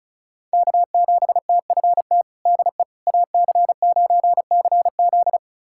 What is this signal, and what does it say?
Morse code, heard as a keyed tone, about 36 wpm, 700 Hz. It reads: K7TFT DE AC9CZ
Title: USA Callsign DE Callsign - 5 Characters - 35wpm